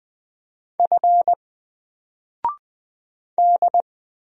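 Morse code, heard as a keyed tone, about 20 wpm, 700 Hz. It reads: F D
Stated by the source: Single letter and number Morse drill at 20 wpm, from N4LCD.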